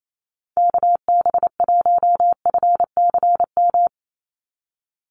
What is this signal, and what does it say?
Morse code, heard as a keyed tone, about 28 wpm, 700 Hz. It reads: KB1FCM
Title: USA Call Signs 28wpm